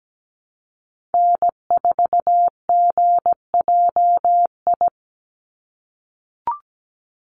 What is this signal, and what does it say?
Morse code, heard as a keyed tone, about 17 wpm, 700 Hz. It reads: N4GJI
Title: USA Call Signs 17wpm